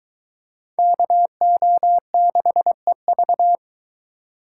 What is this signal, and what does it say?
Morse code, heard as a keyed tone, about 23 wpm, 700 Hz. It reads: KO6EV